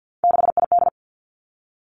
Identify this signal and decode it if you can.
Morse code, heard as a keyed tone, about 50 wpm, 700 Hz. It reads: 6ID
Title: Three Challenging Letter-Number Combinations 50wpm